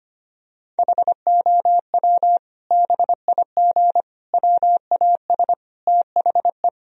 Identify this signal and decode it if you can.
Morse code, heard as a keyed tone, about 25 wpm, 700 Hz. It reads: HOW BIG WAS THE
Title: Sentences from Top 1000 Words - Review Part 4 - 25wpm